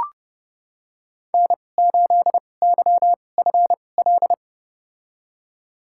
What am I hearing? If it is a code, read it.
Morse code, heard as a keyed tone, about 30 wpm, spaced as for 22 wpm, 700 Hz. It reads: N8YFL
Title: USA Call Signs - 2x Character Spacing - 30wpm